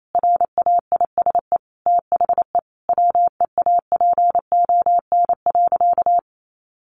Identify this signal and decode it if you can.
Morse code, heard as a keyed tone, about 28 wpm, 700 Hz. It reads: RAISE THE WEAPON.
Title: A Sound Of Thunder 28wpm